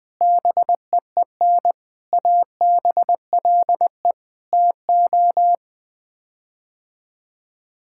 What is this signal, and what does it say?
Morse code, heard as a keyed tone, about 21 wpm, 700 Hz. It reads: BEEN ABLE TO
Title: Sets of 6 Words - 20wpm